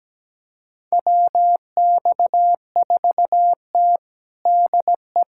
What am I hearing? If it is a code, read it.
Morse code, heard as a keyed tone, about 17 wpm, 700 Hz. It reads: WX4T DE
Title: USA Callsign DE Callsign - 4 Characters - 17wpm